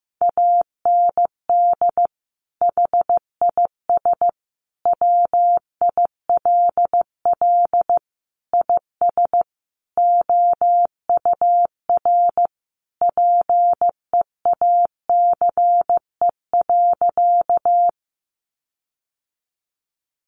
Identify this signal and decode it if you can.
Morse code, heard as a keyed tone, about 15 wpm, 700 Hz. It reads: AND HIS WILL IS OUR PEACE.
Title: Sentences from Top 1500 Words 15wpm